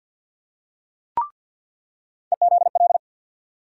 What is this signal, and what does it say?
Morse code, heard as a keyed tone, about 46 wpm, 700 Hz. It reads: EZL
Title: USA Call Sign Postfixes - 3 Letters 50wpm